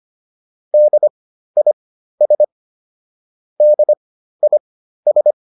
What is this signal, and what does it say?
Morse code, heard as a keyed tone, about 25 wpm, spaced as for 13 wpm, 600 Hz. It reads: DIS DIS